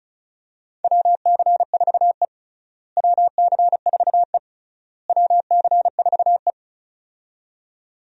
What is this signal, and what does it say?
Morse code, heard as a keyed tone, about 36 wpm, 700 Hz. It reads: WC4E WC4E WC4E
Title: USA Call Signs - 4 Characters - Repeated 3x - 35wpm